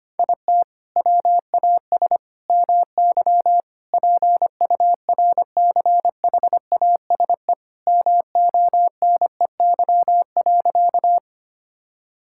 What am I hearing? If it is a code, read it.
Morse code, heard as a keyed tone, about 25 wpm, 700 Hz. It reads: IT WAS MY PURCHASE MONEY.
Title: Sentences from Top 1000 Words - Review Part 5 - 25wpm